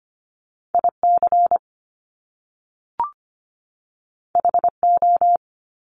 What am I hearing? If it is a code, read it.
Morse code, heard as a keyed tone, about 25 wpm, 700 Hz. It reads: IC HO
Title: Top Two Letter Combinations in Words 25wpm